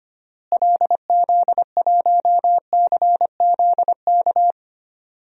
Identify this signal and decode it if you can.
Morse code, heard as a keyed tone, about 25 wpm, 700 Hz. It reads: LZ1CZK
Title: DXCC-100 Call Signs - 6 Characters - Repeated 3x - 25wpm